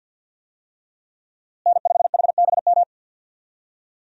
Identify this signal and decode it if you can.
Morse code, heard as a keyed tone, about 52 wpm, 700 Hz. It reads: N5HBM
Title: USA Call Signs - 5 Characters - Repeated 3x - 50wpm